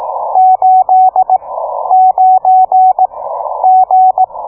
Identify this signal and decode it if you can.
Morse code, heard as a keyed tone, about 18 wpm, 750 Hz, with radio noise. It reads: 8 9 G